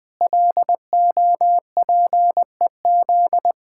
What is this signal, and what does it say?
Morse code, heard as a keyed tone, about 20 wpm, 700 Hz. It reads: LOPEZ
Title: Pangram Challenge v2 - 20wpm